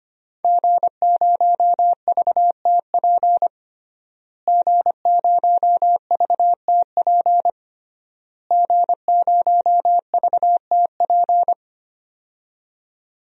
Morse code, heard as a keyed tone, about 25 wpm, 700 Hz. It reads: G0VTP G0VTP G0VTP
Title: DXCC-100 Call Signs - 5 Characters - Repeated 3x - 25wpm